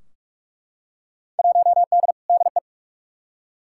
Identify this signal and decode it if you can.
Morse code, heard as a keyed tone, about 46 wpm, 700 Hz. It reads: 1D DE
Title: Field Day Exchanges - 45wpm